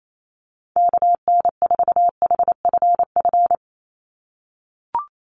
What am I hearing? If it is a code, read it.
Morse code, heard as a keyed tone, about 28 wpm, 700 Hz. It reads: KN4HFF
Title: USA Call Signs 28wpm